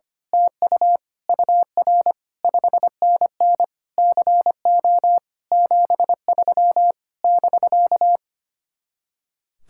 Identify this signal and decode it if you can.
Morse code, heard as a keyed tone, about 24 wpm, 700 Hz. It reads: TU UR 5NN CO 73 <BK>
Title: Speed-Racing Common QSO Phrases - 25wpm